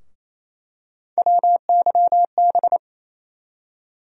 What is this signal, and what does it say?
Morse code, heard as a keyed tone, about 27 wpm, 700 Hz. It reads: WYB
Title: USA Call Sign Postfixes - 3 Letters 28wpm